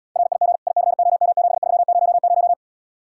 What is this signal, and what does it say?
Morse code, heard as a keyed tone, about 81 wpm, 700 Hz. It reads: RIG ICOM7300